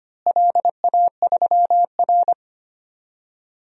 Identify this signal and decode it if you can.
Morse code, heard as a keyed tone, about 25 wpm, 700 Hz. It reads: LA3R
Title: DXCC-100 Call Signs - 4 Characters - Repeated 3x - 25wpm